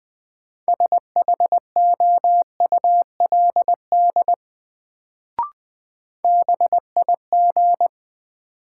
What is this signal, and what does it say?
Morse code, heard as a keyed tone, about 20 wpm, 700 Hz. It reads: SHOULD BIG
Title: Sets of 2 Words - New Word Drill - 20wpm